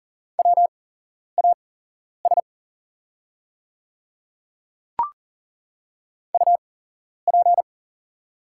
Morse code, heard as a keyed tone, about 40 wpm, spaced as for 11 wpm, 690 Hz. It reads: WAS UP
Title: 8x Character Spacing - Top 100 Words - 40wpm